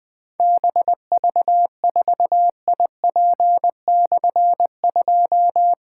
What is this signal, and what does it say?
Morse code, heard as a keyed tone, about 20 wpm, 700 Hz. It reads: BV4IP/2
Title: Difficult Call Signs 20wpm